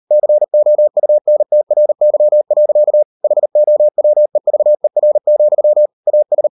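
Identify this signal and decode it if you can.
Morse code, heard as a keyed tone, about 38 wpm, 600 Hz. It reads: COUNTRY. HOWEVER, AS
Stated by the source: Morse code practice at 38 WPM, sending Ken Follett's novel Never in English, from CW med Gnister.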